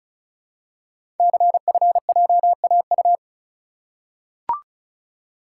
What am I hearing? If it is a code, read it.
Morse code, heard as a keyed tone, about 36 wpm, 690 Hz. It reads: CFJAU
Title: Code Groups - 5 Letters - 35wpm